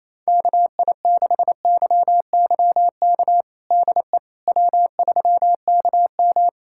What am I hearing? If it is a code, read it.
Morse code, heard as a keyed tone, about 28 wpm, 700 Hz. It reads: KI6YYK DE W3KM
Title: USA Callsign DE Callsign 28wpm